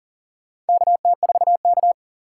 Morse code, heard as a keyed tone, about 40 wpm, 700 Hz. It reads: KT4K
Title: USA Call Signs - 4 Characters - 4x Character Spacing - 40wpm